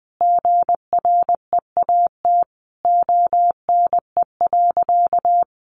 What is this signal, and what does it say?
Morse code, heard as a keyed tone, about 20 wpm, 700 Hz. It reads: GREAT ONE.
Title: Sentences from Top 1300 Words 20wpm